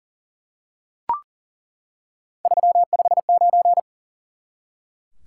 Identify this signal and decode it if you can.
Morse code, heard as a keyed tone, about 40 wpm, 700 Hz. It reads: 359
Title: Top 180 QSO Elements 40wpm